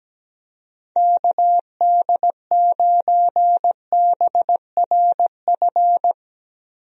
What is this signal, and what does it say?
Morse code, heard as a keyed tone, about 17 wpm, 700 Hz. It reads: KD9BRF
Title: USA Call Signs - 6 Characters - Repeated 2x - 17wpm